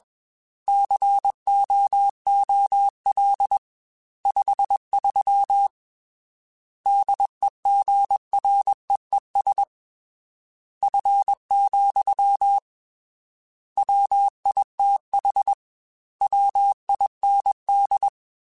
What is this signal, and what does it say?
Morse code, heard as a keyed tone, about 21 wpm, 800 Hz. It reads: COOL 53 DEGREES F, WITH WIND
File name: Weather Sets / Weather Sets-mp3s / Weather 5 1 @20